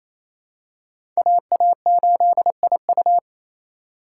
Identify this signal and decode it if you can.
Morse code, heard as a keyed tone, about 27 wpm, 700 Hz. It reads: AA8IU
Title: USA Call Signs - 5 Characters - 4x Character Spacing - 28wpm